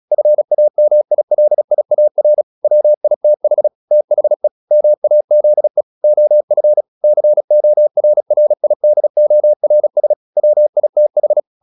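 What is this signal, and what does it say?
Morse code, heard as a keyed tone, about 36 wpm, 600 Hz. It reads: FAMILIAR WITH THE MAZE OF CORRIDORS WITH
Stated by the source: Morse practice at 36 wpm, text of Ken Follett's novel Never, from CW med Gnister.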